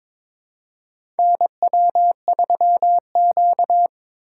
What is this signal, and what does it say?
Morse code, heard as a keyed tone, about 22 wpm, 700 Hz. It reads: NW3Q
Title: USA Call Signs - 4 Characters - 3x Character Spacing - 22wpm